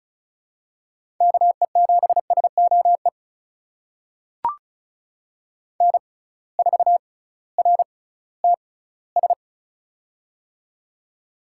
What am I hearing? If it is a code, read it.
Morse code, heard as a keyed tone, about 34 wpm, spaced as for 19 wpm, 700 Hz. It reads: KE7SOE N4RTS
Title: USA Call Signs - 6x Character Spacing - 35wpm